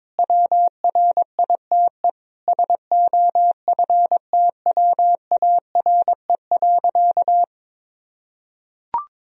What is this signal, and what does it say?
Morse code, heard as a keyed tone, about 22 wpm, 700 Hz. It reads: WRITE SOFTWARE.
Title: Sentences from Top 1000 Words - Review Part 3 - 22wpm